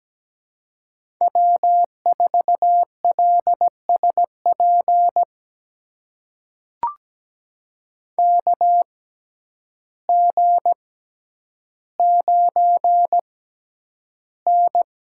Morse code, heard as a keyed tone, about 17 wpm, spaced as for 9 wpm, 700 Hz. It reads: W4LSP KG9N
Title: USA Call Signs - 6x Character Spacing - 17wpm